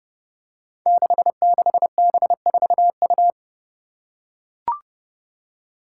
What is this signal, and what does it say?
Morse code, heard as a keyed tone, about 30 wpm, 700 Hz. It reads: 66B4U
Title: Five Challenging Letter-Number Combinations 30wpm